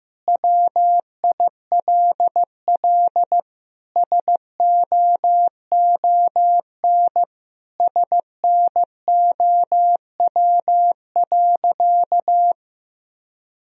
Morse code, heard as a keyed tone, about 15 wpm, 700 Hz. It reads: WILL SOON SNOW.
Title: Sentences from Top 1000 Words - Review Part 3 - 15wpm